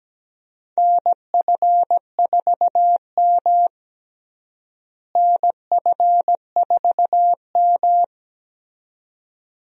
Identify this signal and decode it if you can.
Morse code, heard as a keyed tone, about 17 wpm, 700 Hz. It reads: NF4M NF4M
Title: USA Call Signs - 4 Characters - Repeated 2x - 17wpm